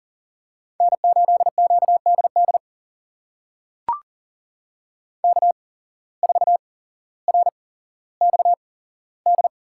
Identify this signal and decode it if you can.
Morse code, heard as a keyed tone, about 41 wpm, spaced as for 17 wpm, 700 Hz. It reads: N8QDD K4RXD
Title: USA Call Signs - 5 Characters - 8x Character Spacing - 40wpm